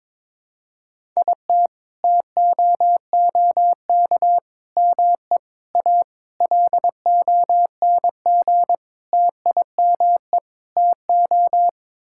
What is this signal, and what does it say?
Morse code, heard as a keyed tone, about 22 wpm, 700 Hz. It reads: IT TOOK ME A LONG TIME TO
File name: practice / 22wpm / Sets of 7 Words - 22wpm